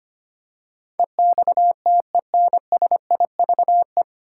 Morse code, heard as a keyed tone, about 25 wpm, 700 Hz. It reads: EXTENSIVE
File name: Nine Letter Words - 25wpm